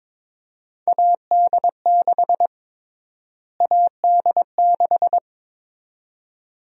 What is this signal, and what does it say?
Morse code, heard as a keyed tone, about 22 wpm, 700 Hz. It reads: AD6 AD6
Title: USA Call Sign Prefixes - Repeated 2x - 22wpm